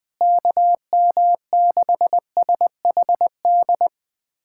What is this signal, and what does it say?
Morse code, heard as a keyed tone, about 19 wpm, 700 Hz. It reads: KM6SHD
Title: USA Callsign DE Callsign 20wpm